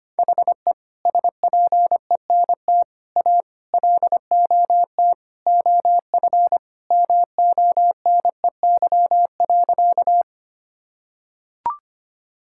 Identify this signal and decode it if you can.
Morse code, heard as a keyed tone, about 25 wpm, 700 Hz. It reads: HE SPENT A LOT OF MONEY.
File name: practice / 25wpm / Sentences with 4-Word Phrases - 25wpm